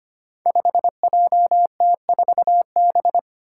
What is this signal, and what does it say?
Morse code, heard as a keyed tone, about 25 wpm, 700 Hz. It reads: 5JT4B
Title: Five Challenging Letter-Number Combinations 25wpm